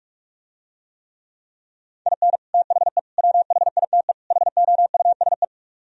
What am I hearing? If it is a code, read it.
Morse code, heard as a keyed tone, about 44 wpm, 700 Hz. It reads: IN THE WHITE HOUSE